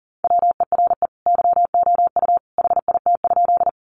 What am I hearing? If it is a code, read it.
Morse code, heard as a keyed tone, about 40 wpm, 700 Hz. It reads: WERE YOU HIT?
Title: Sentences from Top 1200 Words 40wpm